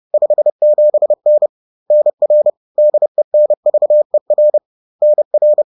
Morse code, heard as a keyed tone, about 30 wpm, 600 Hz. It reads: 57N NR DENVER NR
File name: 30wpm